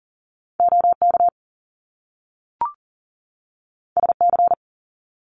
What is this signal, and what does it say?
Morse code, heard as a keyed tone, about 42 wpm, 700 Hz. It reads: OK SC
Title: US States 40wpm